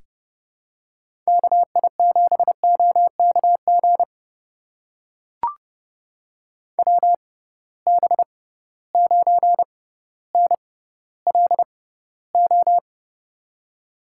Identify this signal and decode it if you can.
Morse code, heard as a keyed tone, about 30 wpm, spaced as for 15 wpm, 700 Hz. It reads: KI7OKG WB9NLO
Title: USA Call Signs - 6 Characters - 6x Character Spacing - 30wpm